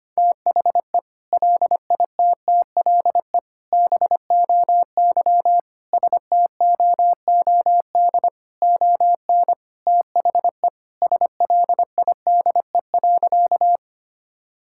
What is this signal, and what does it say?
Morse code, heard as a keyed tone, about 25 wpm, 700 Hz. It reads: THE LITTLE BOY STOOD ON THE SLIDE.
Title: Sentences from Top 1600 Words 25wpm